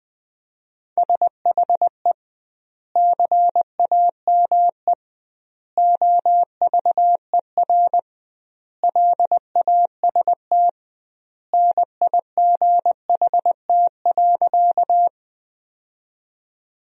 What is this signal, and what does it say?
Morse code, heard as a keyed tone, about 20 wpm, 700 Hz. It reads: SHE CAME OVER LAST NIGHT.
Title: 2x Word-Spacing - Sentences from Top 300 Words - 20wpm